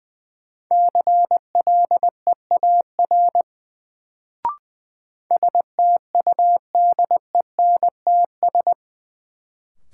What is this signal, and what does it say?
Morse code, heard as a keyed tone, about 21 wpm, 700 Hz. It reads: CLEAR STUDENTS
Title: Sets of 5 Words Encore - New Word Drill - 20wpm